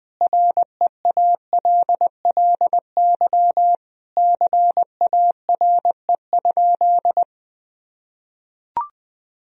Morse code, heard as a keyed tone, about 21 wpm, 700 Hz. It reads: REALLY CARE?
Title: Sentences from Top 1000 Words - Review Part 3 - 20wpm